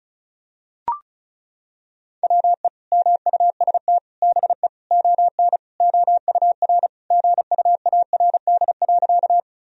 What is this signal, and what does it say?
Morse code, heard as a keyed tone, about 34 wpm, 700 Hz. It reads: WE MUST BE ON OUR GUARD.
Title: Sentences from Top 1000 Words - Review Part 4 - 35wpm